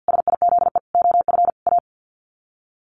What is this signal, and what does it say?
Morse code, heard as a keyed tone, about 50 wpm, 700 Hz. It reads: SIZE OF A
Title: Sets of 4 Words 50wpm